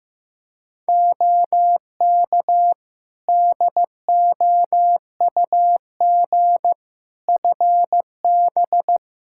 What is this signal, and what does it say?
Morse code, heard as a keyed tone, about 16 wpm, 700 Hz. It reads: OK DOUG FB
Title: Practice QSO - Exchange 2 - Info - 15wpm